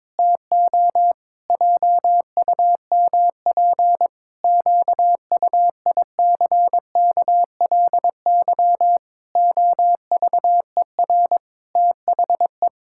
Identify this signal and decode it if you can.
Morse code, heard as a keyed tone, about 22 wpm, 700 Hz. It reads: TO JUMP QUICKLY OVER THE
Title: Pangram Challenge v2 - 22wpm